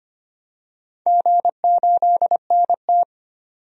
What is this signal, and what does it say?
Morse code, heard as a keyed tone, about 25 wpm, 700 Hz. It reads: G8NT
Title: DXCC-100 Call Signs - 4 Characters - 25wpm